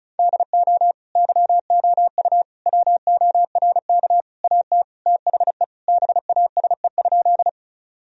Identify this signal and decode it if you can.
Morse code, heard as a keyed tone, about 36 wpm, 700 Hz. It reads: DO YOU WORK AT THE BASE?
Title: Sentences from Top 1000 Words - Review Part 1 - 35wpm